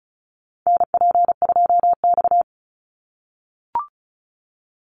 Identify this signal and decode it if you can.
Morse code, heard as a keyed tone, about 34 wpm, 700 Hz. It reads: NP2X
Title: Call Signs - 35wpm